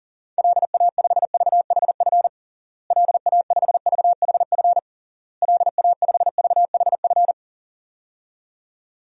Morse code, heard as a keyed tone, about 40 wpm, 700 Hz. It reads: LA5VHF LA5VHF LA5VHF
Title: DXCC-100 Call Signs - 6 Characters - Repeated 3x - 40wpm